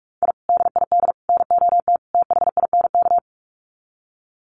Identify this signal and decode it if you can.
Morse code, heard as a keyed tone, about 44 wpm, 700 Hz. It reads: I DID NOT THINK
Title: Sets of 4 Words 45wpm